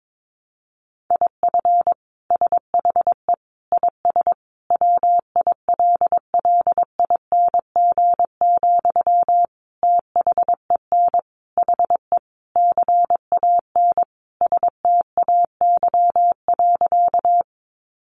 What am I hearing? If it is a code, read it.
Morse code, heard as a keyed tone, about 22 wpm, 700 Hz. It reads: IF SHE IS WILLING, THEN HE CAN STAY.
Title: Sentences from Top 1000 Words 22wpm